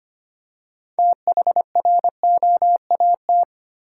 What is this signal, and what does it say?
Morse code, heard as a keyed tone, about 25 wpm, 700 Hz. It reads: THROAT
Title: Six Letter Words - 25wpm